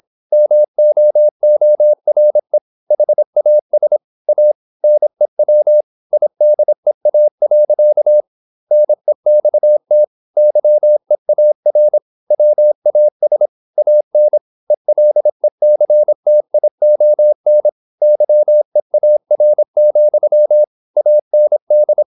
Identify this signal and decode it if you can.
Morse code, heard as a keyed tone, about 26 wpm, 600 Hz. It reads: MOORE HAS A NEW IDEA. NEXT YEAR WAS AN ELECTION YEAR, AND
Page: Never 26wpm | CW med Gnister